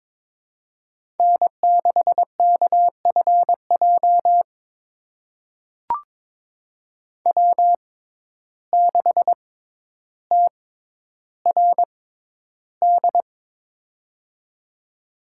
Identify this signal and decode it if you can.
Morse code, heard as a keyed tone, about 22 wpm, spaced as for 11 wpm, 700 Hz. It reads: N6KFJ W6TRD
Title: USA Call Signs - 5 Characters - 6x Character Spacing - 22wpm